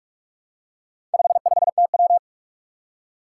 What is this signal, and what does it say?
Morse code, heard as a keyed tone, about 44 wpm, 700 Hz. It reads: 55TW
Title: Four Challenging Letter-Number Combinations 45wpm